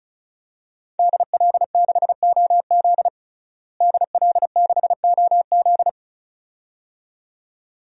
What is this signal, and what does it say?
Morse code, heard as a keyed tone, about 34 wpm, 700 Hz. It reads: DL6OZ DL6OZ
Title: DXCC-100 Call Signs - 5 Characters - Repeated 2x - 35wpm